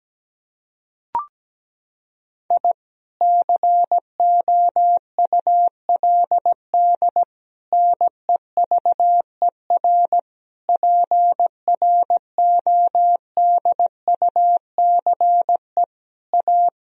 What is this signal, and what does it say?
Morse code, heard as a keyed tone, about 17 wpm, 700 Hz. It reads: I COULD NEVER PRODUCE A
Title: Sentences from Top 1000 Words - Review Part 3 - 17wpm